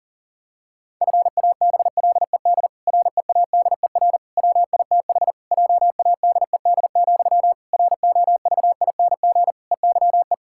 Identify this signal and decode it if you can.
Morse code, heard as a keyed tone, about 40 wpm, 700 Hz. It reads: FABLED READER WITH JADED, ROVING EYE
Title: Pangram Challenge v2 - 40wpm